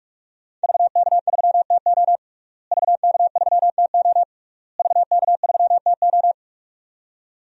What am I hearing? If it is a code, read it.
Morse code, heard as a keyed tone, about 46 wpm, 700 Hz. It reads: VK3TO VK3TO VK3TO
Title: DXCC-100 Call Signs - 5 Characters - Repeated 3x - 45wpm